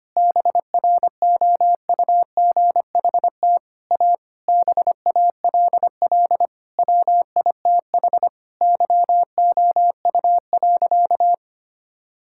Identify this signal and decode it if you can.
Morse code, heard as a keyed tone, about 25 wpm, 700 Hz. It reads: BROUGHT A BALL WITH YOU.
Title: Sentences from Top 1000 Words - Review Part 6 - 25wpm